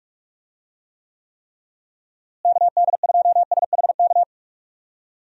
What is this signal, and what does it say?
Morse code, heard as a keyed tone, about 42 wpm, 700 Hz. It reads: KD2SHK
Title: USA Call Signs - 6 Characters - Repeated 3x - 45wpm